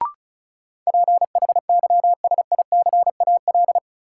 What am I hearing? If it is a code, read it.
Morse code, heard as a keyed tone, about 36 wpm, 700 Hz. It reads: PHYSICAL
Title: Sentences from Top 1400 Words - New Word Drill Alt - 35wpm